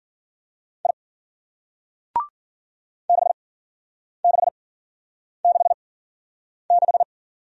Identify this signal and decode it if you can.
Morse code, heard as a keyed tone, about 56 wpm, spaced as for 9 wpm, 700 Hz. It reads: I 6666